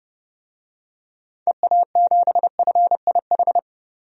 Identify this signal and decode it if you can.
Morse code, heard as a keyed tone, about 30 wpm, 700 Hz. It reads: EA7FIH
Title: DXCC-100 Call Signs - 6 Characters - 30wpm